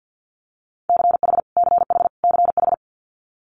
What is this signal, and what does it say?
Morse code, heard as a keyed tone, about 50 wpm, 690 Hz. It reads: CH CH CH
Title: Multi-letter Phonemes Repeated 3x 50wpm